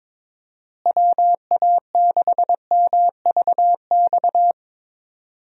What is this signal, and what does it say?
Morse code, heard as a keyed tone, about 22 wpm, 700 Hz. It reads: WA6MVX
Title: USA Call Signs - 6 Characters - Repeated 2x - 22wpm